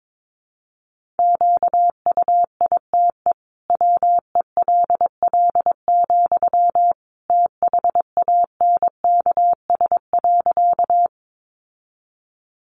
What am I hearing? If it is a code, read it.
Morse code, heard as a keyed tone, about 22 wpm, 700 Hz. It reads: QUITE WELL, THANKS.
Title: Sentences from Top 700 Words 22wpm